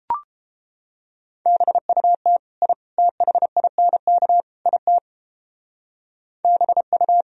Morse code, heard as a keyed tone, about 33 wpm, 700 Hz. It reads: BUT I THINK IT BU